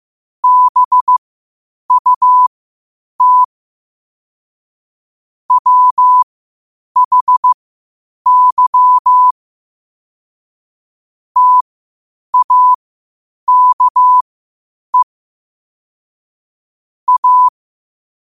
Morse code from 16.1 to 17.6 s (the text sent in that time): A